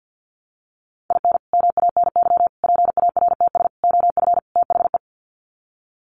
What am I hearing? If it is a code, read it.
Morse code, heard as a keyed tone, about 50 wpm, 700 Hz. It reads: IN MANY PARTS OF THE
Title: Sets of 5 Words Encore 50wpm